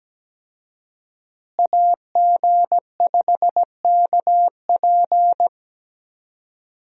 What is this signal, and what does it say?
Morse code, heard as a keyed tone, about 17 wpm, 700 Hz. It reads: AG5KP
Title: USA Call Signs - 5 Characters - Repeated 2x - 17wpm